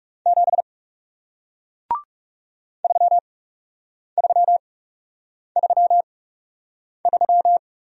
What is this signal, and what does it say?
Morse code, heard as a keyed tone, about 40 wpm, spaced as for 8 wpm, 700 Hz. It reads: 7 3333